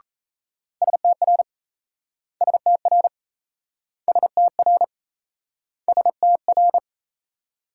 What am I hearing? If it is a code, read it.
Morse code, heard as a keyed tone, about 37 wpm, 700 Hz. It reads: STR STR STR STR